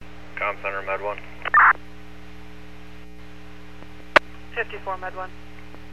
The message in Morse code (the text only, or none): none